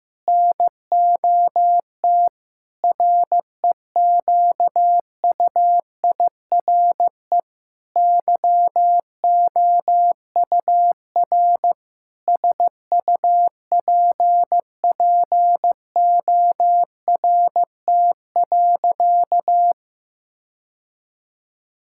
Morse code, heard as a keyed tone, about 14 wpm, 700 Hz. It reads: NOT REQUIRE YOUR SUPPORT.
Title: Sentences from Top 1000 Words - Review Part 6 - 15wpm